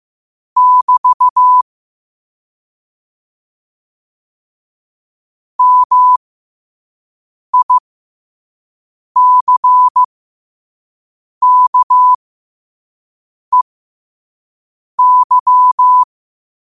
Morse code, heard as a keyed tone, about 15 wpm, spaced as for 5 wpm, 1000 Hz. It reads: <BT> MICKEY
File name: qotdfiles / 5wpm